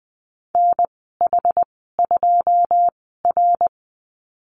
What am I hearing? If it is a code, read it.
Morse code, heard as a keyed tone, about 20 wpm, spaced as for 15 wpm, 700 Hz. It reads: NH2R
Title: 122 Lesson 20g - Number 3 - Call Signs 20wpm